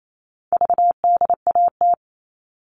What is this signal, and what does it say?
Morse code, heard as a keyed tone, about 28 wpm, 700 Hz. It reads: VDAT
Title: All Cut Numbers - 4-Digit 28wpm